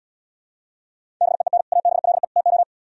Morse code, heard as a keyed tone, about 77 wpm, 700 Hz. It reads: BEEN ABLE TO